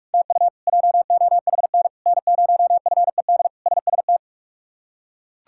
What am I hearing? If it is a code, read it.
Morse code, heard as a keyed tone, about 42 wpm, 700 Hz. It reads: TU JOHN N0FED SST
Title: K1USN SST QSO Practice - 45wpm